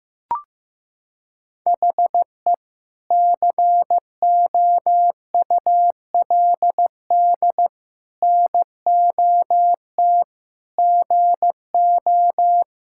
Morse code, heard as a keyed tone, about 15 wpm, 700 Hz. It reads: HE COULD NOT GO
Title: Sentences from Top 1000 Words - Review Part 6 - 15wpm